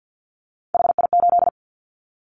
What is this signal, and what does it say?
Morse code, heard as a keyed tone, about 50 wpm, 700 Hz. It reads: HI8
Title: Callsign Prefixes for Top 100 DXCC - 50wpm